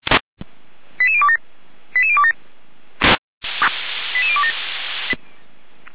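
Five-tone selective call (ZVEI), heard as quick runs of five tones, tones of about 70 ms. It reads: 88027 88027 88027
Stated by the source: ZVEI five-tone call received by SDR.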